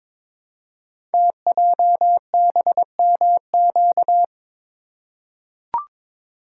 Morse code, heard as a keyed tone, about 22 wpm, 700 Hz. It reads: TJBMQ
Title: Code Groups - 5 Letters - 22wpm